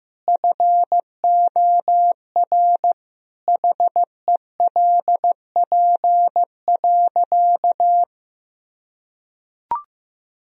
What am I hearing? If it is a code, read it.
Morse code, heard as a keyed tone, about 15 wpm, 700 Hz. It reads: FOR HELP.
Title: Sentences from Top 1000 Words - Review Part 2 - 15wpm